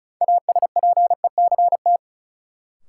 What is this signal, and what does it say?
Morse code, heard as a keyed tone, about 34 wpm, 700 Hz. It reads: ASPECT
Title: Sentences from Top 1300 Words - New Word Drill Alt - 35wpm